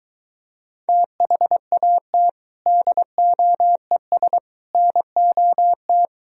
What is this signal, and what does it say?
Morse code, heard as a keyed tone, about 23 wpm, 700 Hz. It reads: THAT DOES NOT